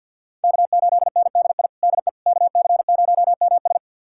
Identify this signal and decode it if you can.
Morse code, heard as a keyed tone, about 49 wpm, 700 Hz. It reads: K8NDI DE KC0MS
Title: USA Callsign DE Callsign - 5 Characters - 50wpm